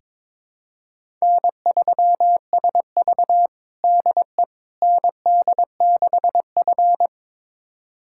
Morse code, heard as a keyed tone, about 22 wpm, 700 Hz. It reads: N3SV DE ND6F
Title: USA Callsign DE Callsign - 4 Characters - 22wpm